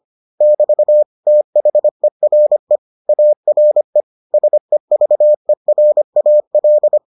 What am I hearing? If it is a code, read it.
Morse code, heard as a keyed tone, about 25 wpm, 600 Hz. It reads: <BT> THERE ARE SEVERAL